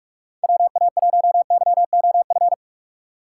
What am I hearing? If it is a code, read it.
Morse code, heard as a keyed tone, about 46 wpm, 700 Hz. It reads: WA1YOF
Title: USA Call Signs - 6 Characters - 2x Character Spacing - 45wpm